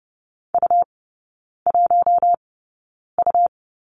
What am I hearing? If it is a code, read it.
Morse code, heard as a keyed tone, about 30 wpm, spaced as for 9 wpm, 690 Hz. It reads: U1U
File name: Lesson 17e - Letter u - Three Characters 30wpm